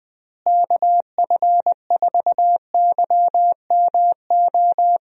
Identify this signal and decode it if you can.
Morse code, heard as a keyed tone, about 20 wpm, 700 Hz. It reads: KF4YMO
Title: USA Call Signs - 6 Characters - Repeated 3x - 20wpm